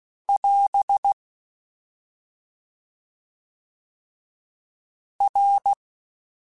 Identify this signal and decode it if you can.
Morse code, heard as a keyed tone, about 16 wpm, 800 Hz. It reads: <AS> R